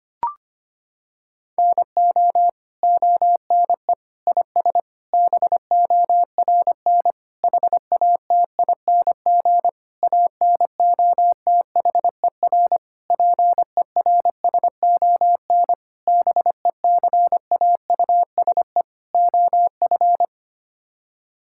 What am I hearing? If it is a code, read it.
Morse code, heard as a keyed tone, about 25 wpm, 700 Hz. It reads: NO ONE IS BORN HATING ANOTHER PERSON BECAUSE OF
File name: Sets of 9 Words - 25wpm